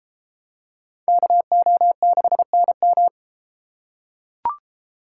Morse code, heard as a keyed tone, about 32 wpm, 700 Hz. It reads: KO6NM